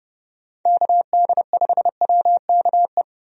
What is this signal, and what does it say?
Morse code, heard as a keyed tone, about 32 wpm, 700 Hz. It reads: KD5WKE
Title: USA Call Signs - 6 Characters - 30wpm